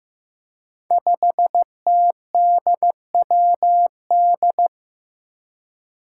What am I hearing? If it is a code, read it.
Morse code, heard as a keyed tone, about 15 wpm, 700 Hz. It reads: 5TDWD
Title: Five Challenging Letter-Number Combinations 15wpm